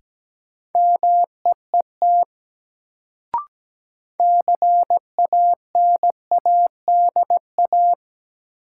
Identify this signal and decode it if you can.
Morse code, heard as a keyed tone, about 17 wpm, 700 Hz. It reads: MEET CANADA
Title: Sets of 5 Words - New Word Drill - 17wpm